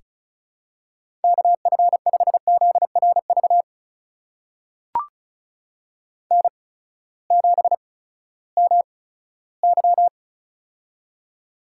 Morse code, heard as a keyed tone, about 36 wpm, spaced as for 18 wpm, 700 Hz. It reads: KF5ZRV N7MY
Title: USA Call Signs - 8x Character Spacing - 35wpm